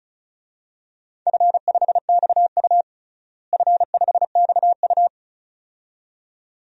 Morse code, heard as a keyed tone, about 36 wpm, 700 Hz. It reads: F5XU F5XU
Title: DXCC-100 Call Signs - 4 Characters - Repeated 2x - 35wpm